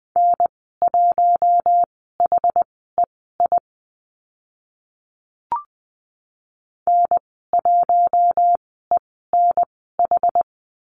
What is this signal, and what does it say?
Morse code, heard as a keyed tone, about 20 wpm, spaced as for 14 wpm, 700 Hz. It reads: N1HEI N1ENH
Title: 062 Lesson 11g - Letter h - Call Signs 20wpm